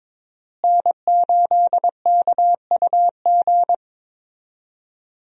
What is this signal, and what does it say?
Morse code, heard as a keyed tone, about 22 wpm, 700 Hz. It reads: N8KUG
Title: USA Call Signs - 5 Characters - Repeated 3x - 22wpm